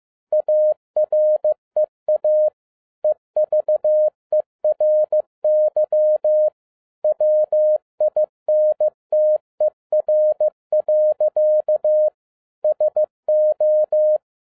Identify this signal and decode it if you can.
Morse code, heard as a keyed tone, about 15 wpm, 610 Hz. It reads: AREA EVERY WINTER. SO
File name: FIELD DAY CONTEST QRV FROM SAKAE VILLAGE 15wpm